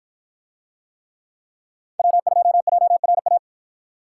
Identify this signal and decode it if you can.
Morse code, heard as a keyed tone, about 53 wpm, 700 Hz. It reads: W2JRA